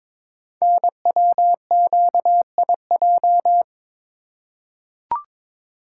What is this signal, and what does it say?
Morse code, heard as a keyed tone, about 23 wpm, 690 Hz. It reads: NWQIJ
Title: Code Groups - 5 Letter:Numbers - 22wpm